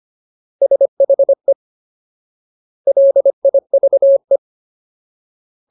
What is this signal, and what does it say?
Morse code, heard as a keyed tone, about 25 wpm, 550 Hz. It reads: SHE LIVE
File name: Ultimate Word List / 25WPM Ultimate